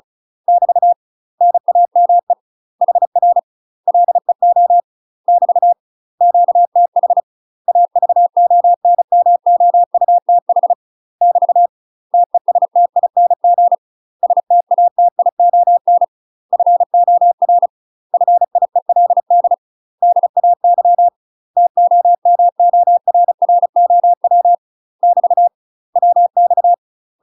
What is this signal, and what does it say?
Morse code, heard as a keyed tone, about 34 wpm, 700 Hz. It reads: <BT> NAME HR LEO <BT> QTH AVONMOUTH <BT> TESTING STATION FOR FIELD DAY TOMORROW <BT> WX
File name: UK MASTER QSOs TRAINING / 35WPM / UK MASTER QSOs 0032